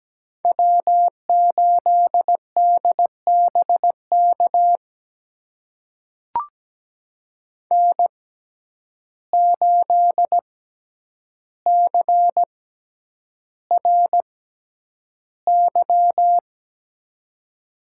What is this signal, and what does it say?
Morse code, heard as a keyed tone, about 17 wpm, spaced as for 9 wpm, 700 Hz. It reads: W8DBK N8CRY
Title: USA Call Signs - 6x Character Spacing - 17wpm